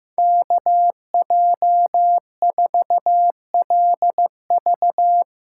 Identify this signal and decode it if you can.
Morse code, heard as a keyed tone, about 15 wpm, 700 Hz. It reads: KJ4LV
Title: USA Call Signs - 6x Character Spacing - 15wpm